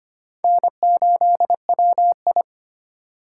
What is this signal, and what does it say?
Morse code, heard as a keyed tone, about 33 wpm, 700 Hz. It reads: N8WI